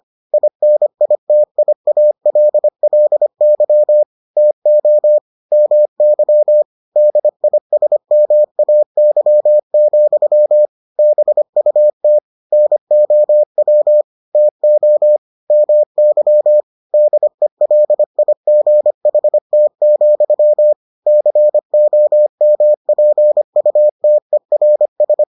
Morse code, heard as a keyed tone, about 25 wpm, 600 Hz. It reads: INITIALLY TO MY DISMAY, BUT NOW TO MY DELIGHT, COMPUTERS